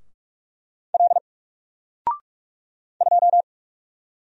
Morse code, heard as a keyed tone, about 44 wpm, 700 Hz. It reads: L 2
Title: Single Letter-Number Alt 45wpm